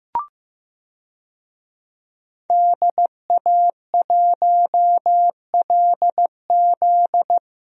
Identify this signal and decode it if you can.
Morse code, heard as a keyed tone, about 15 wpm, 700 Hz. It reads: DA1LZ
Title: USA Call Signs - 5 Characters - 15wpm